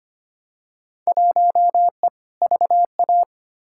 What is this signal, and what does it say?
Morse code, heard as a keyed tone, about 25 wpm, 700 Hz. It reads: 1E VA
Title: Field Day Exchanges - 25wpm